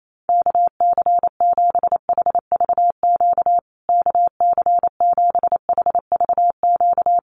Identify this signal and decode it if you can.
Morse code, heard as a keyed tone, about 28 wpm, 700 Hz. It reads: KC7HVQ KC7HVQ
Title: USA Call Signs Repeated 3x 28wpm